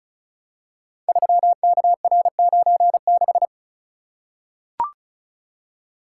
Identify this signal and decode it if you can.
Morse code, heard as a keyed tone, about 36 wpm, 700 Hz. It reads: 3KR96
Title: Code Groups - 5 Letter:Numbers - 35wpm